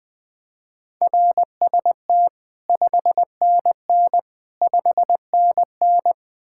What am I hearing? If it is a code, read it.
Morse code, heard as a keyed tone, about 21 wpm, 700 Hz. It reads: RST 5NN 5NN
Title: Speed-Racing Common QSO Phrases - 20wpm